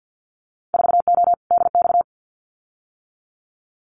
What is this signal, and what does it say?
Morse code, heard as a keyed tone, about 50 wpm, 700 Hz. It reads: 4O DX
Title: Winter Field Day Exchanges Encore 50wpm